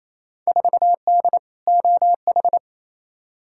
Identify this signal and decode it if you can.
Morse code, heard as a keyed tone, about 28 wpm, 700 Hz. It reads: 4D OH
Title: Field Day Exchanges Encore - 28wpm